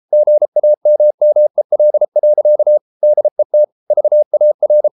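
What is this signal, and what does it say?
Morse code, heard as a keyed tone, about 33 wpm, 600 Hz. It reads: GAMMEL. DET VAR
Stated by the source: Morse code at 33 wpm, QRQ.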